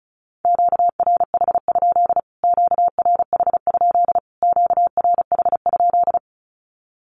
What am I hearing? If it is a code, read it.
Morse code, heard as a keyed tone, about 34 wpm, 700 Hz. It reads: QRH? QRH? QRH?
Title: Q-Codes Repeated 3x 35wpm